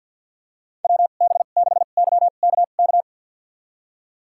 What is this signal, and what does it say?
Morse code, heard as a keyed tone, about 52 wpm, spaced as for 37 wpm, 700 Hz. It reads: WB6YKK
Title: USA Call Signs - 2x Character Spacing - 50wpm